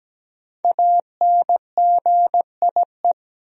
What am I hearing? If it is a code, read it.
Morse code, heard as a keyed tone, about 17 wpm, 700 Hz. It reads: ANGIE
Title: USA Common Names - Max 5 Letters - 17wpm